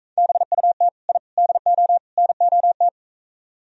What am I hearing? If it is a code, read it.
Morse code, heard as a keyed tone, about 44 wpm, 700 Hz. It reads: BUT I DO NOT